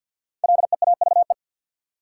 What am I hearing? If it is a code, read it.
Morse code, heard as a keyed tone, about 51 wpm, 700 Hz. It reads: LEAVE
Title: Sentences from Top 400 Words - New Word Drill - 50wpm